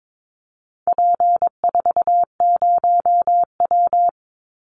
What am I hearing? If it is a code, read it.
Morse code, heard as a keyed tone, about 22 wpm, 700 Hz. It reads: P40W
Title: Call Signs Repeated 3x - 22wpm